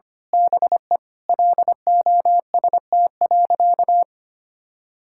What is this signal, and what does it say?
Morse code, heard as a keyed tone, about 25 wpm, 700 Hz. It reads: BE LOST.
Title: Sentences from Top 1600 Words 25wpm